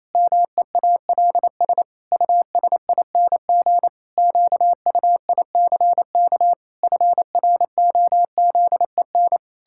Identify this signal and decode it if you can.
Morse code, heard as a keyed tone, about 28 wpm, 700 Hz. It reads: MEALS USING QUICK FROZEN
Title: Pangram Challenge v2 - 28wpm